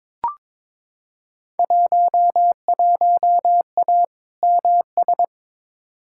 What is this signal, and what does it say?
Morse code, heard as a keyed tone, about 22 wpm, 700 Hz. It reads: 11A MS
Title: Field Day Exchanges Encore - 22wpm